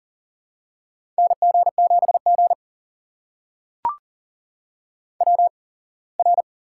Morse code, heard as a keyed tone, about 41 wpm, spaced as for 24 wpm, 700 Hz. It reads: NG7G WR
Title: USA Call Signs - 4 Characters - 8x Character Spacing - 40wpm